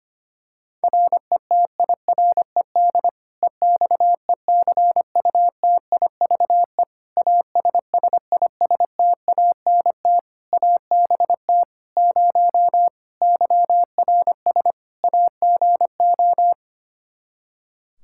Morse code, heard as a keyed tone, about 25 wpm, 700 Hz. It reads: RETIRED EXECUTIVE ASSISTANT ABT 0 YRS AGO
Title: K1USN SST QSO Practice 25wpm